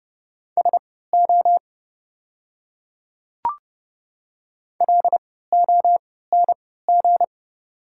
Morse code, heard as a keyed tone, about 31 wpm, spaced as for 17 wpm, 690 Hz. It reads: SO LONG
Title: 3x Character Spacing - Top 100 Words - 30wpm